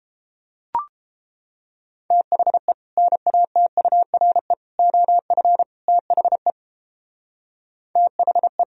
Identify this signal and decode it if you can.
Morse code, heard as a keyed tone, about 29 wpm, 700 Hz. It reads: THE NATURE OF THE THE